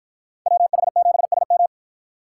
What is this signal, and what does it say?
Morse code, heard as a keyed tone, about 49 wpm, 700 Hz. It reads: WH7SM